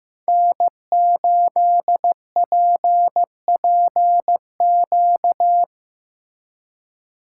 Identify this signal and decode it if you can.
Morse code, heard as a keyed tone, about 17 wpm, 700 Hz. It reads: N8PPQ
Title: Speed Racing - USA 1x3 Call Signs 15wpm